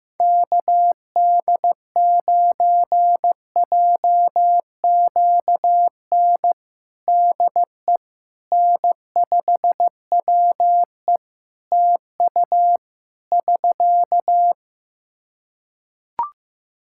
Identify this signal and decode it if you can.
Morse code, heard as a keyed tone, about 15 wpm, 700 Hz. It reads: KD9JQN DE N5WE TU <SK>
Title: Practice QSO - Exchange 4 - Ending - 15wpm